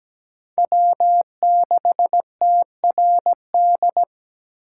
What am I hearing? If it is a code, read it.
Morse code, heard as a keyed tone, about 17 wpm, 700 Hz. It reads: W6TRD
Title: USA Call Signs - 3x Character Spacing - 17wpm